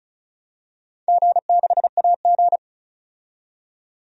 Morse code, heard as a keyed tone, about 36 wpm, 700 Hz. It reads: G6AG
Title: DXCC-100 Call Signs - 4 Characters - 35wpm